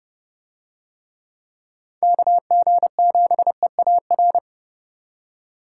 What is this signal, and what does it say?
Morse code, heard as a keyed tone, about 29 wpm, 700 Hz. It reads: KG7EAR